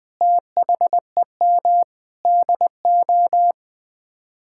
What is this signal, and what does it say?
Morse code, heard as a keyed tone, about 20 wpm, 700 Hz. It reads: THEM DO
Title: Sets of 9 Words - 20wpm